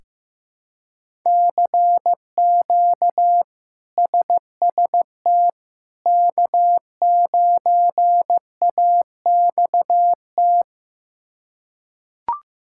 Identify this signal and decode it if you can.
Morse code, heard as a keyed tone, about 15 wpm, 700 Hz. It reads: CQ SST K9AXT
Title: K1USN SST QSO Practice - 15wpm